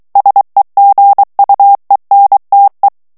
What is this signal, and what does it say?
Morse code, heard as a keyed tone, about 23 wpm, 800 Hz. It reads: SEGUENTE
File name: ITALIEN texte clair 2 vitesse 22 mots minute